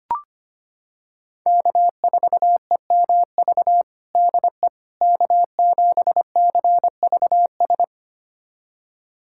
Morse code, heard as a keyed tone, about 25 wpm, 700 Hz. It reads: K4EMV DE K7CVS
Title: USA Callsign DE Callsign - 5 Characters - 25wpm